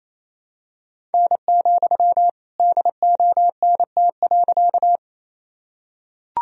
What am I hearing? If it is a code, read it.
Morse code, heard as a keyed tone, about 28 wpm, 700 Hz. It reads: N, DONT.
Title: Sentences from Top 300 Words and Abbreviations 28wpm